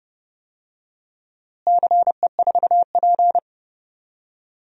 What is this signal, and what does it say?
Morse code, heard as a keyed tone, about 29 wpm, 700 Hz. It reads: CE4P
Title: DXCC-100 Call Signs - 4 Characters - Repeated 3x - 30wpm